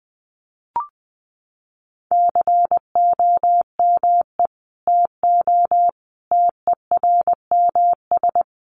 Morse code, heard as a keyed tone, about 20 wpm, 700 Hz. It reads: COME TO TERMS
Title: Sets of 5 Words Encore2 20wpm